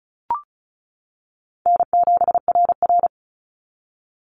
Morse code, heard as a keyed tone, about 36 wpm, 700 Hz. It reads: N7RR
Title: NA SOTA Activator Callsigns 35wpm